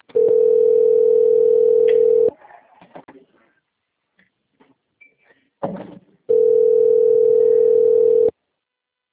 Ringback tone (North America), 440 + 480 Hz.
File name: out- - - - - .30343